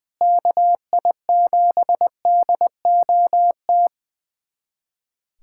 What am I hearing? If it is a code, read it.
Morse code, heard as a keyed tone, about 20 wpm, 700 Hz. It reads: KI7DOT
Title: USA Callsign DE Callsign - 6 Characters - 20wpm